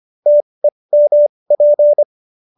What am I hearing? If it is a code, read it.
Morse code, heard as a keyed tone, about 25 wpm, spaced as for 20 wpm, 590 Hz. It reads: TEMP
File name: QSO301 20